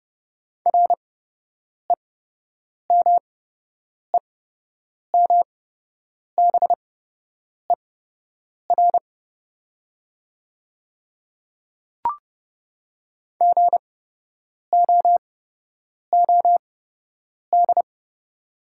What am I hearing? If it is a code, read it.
Morse code, heard as a keyed tone, about 30 wpm, spaced as for 8 wpm, 700 Hz. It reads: REMEMBER GOOD
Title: 8x Character Spacing - Top 500 Words - 30wpm